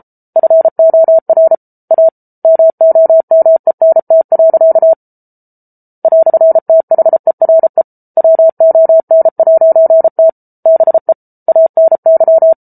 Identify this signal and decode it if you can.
Morse code, heard as a keyed tone, about 34 wpm, 650 Hz. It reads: FOR A MOMENT. "THERE WON'T BE ANY